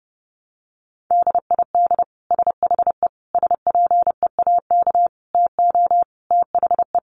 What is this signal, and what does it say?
Morse code, heard as a keyed tone, about 30 wpm, 700 Hz. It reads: DID SHE SPEAK TO THE
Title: Sentences from Top 1200 Words 30wpm